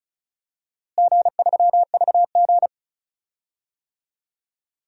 Morse code, heard as a keyed tone, about 34 wpm, 700 Hz. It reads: G3VG
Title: DXCC-100 Call Signs - 4 Characters - Repeated 2x - 35wpm